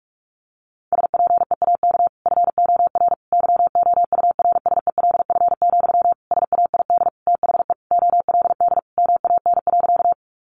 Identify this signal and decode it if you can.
Morse code, heard as a keyed tone, about 44 wpm, 700 Hz. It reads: SPEAK FOR YOURSELF, SAID THE OLD MAN.
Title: Sentences from Top 1100 Words 45wpm